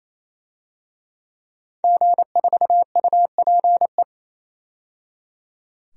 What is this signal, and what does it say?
Morse code, heard as a keyed tone, about 28 wpm, 700 Hz. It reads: G4UPE
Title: DXCC-100 Call Signs - 5 Characters - 28wpm